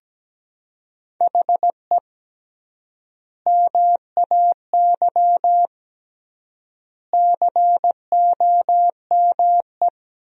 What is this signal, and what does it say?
Morse code, heard as a keyed tone, about 17 wpm, 700 Hz. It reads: HE MAY COME
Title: 3x Word-Spacing - Sentences from Top 300 Words - 17wpm